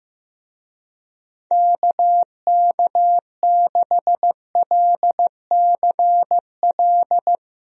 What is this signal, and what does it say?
Morse code, heard as a keyed tone, about 15 wpm, 700 Hz. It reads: KK6LCL
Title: USA Call Signs - 6 Characters - Repeated 2x - 15wpm